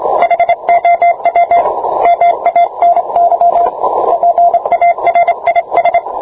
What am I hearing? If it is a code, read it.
Morse code, heard as a keyed tone, about 27 wpm, 690 Hz, with radio noise. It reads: HOW MANY PARIS